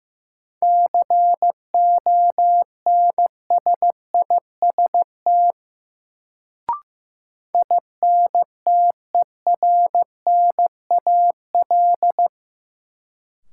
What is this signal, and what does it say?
Morse code, heard as a keyed tone, about 15 wpm, 700 Hz. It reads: CONSIST INTERNAL
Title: Sentences from Top 1400 Words - New Word Drill - 15wpm